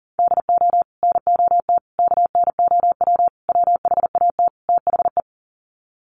Morse code, heard as a keyed tone, about 37 wpm, 700 Hz. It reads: DO NOT KNOW WHAT THE
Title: Sets of 5 Words 40wpm